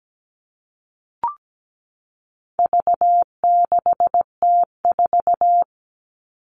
Morse code, heard as a keyed tone, about 17 wpm, 700 Hz. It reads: V6T4
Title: All Cut Numbers - 4-Digit 17wpm